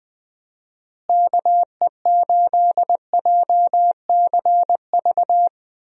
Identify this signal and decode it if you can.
Morse code, heard as a keyed tone, about 20 wpm, 700 Hz. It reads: KE8JCV
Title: USA Call Signs - 6 Characters - Repeated 2x - 20wpm